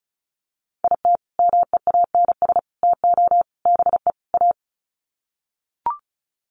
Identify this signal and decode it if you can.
Morse code, heard as a keyed tone, about 34 wpm, 700 Hz. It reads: IT MEANS TO BE A
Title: Sets of 5 Words Encore2 35wpm